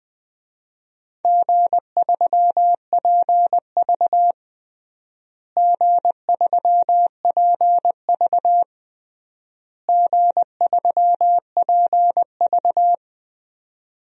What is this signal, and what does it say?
Morse code, heard as a keyed tone, about 21 wpm, 700 Hz. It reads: G3PV G3PV G3PV
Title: DXCC-100 Call Signs - 4 Characters - Repeated 3x - 20wpm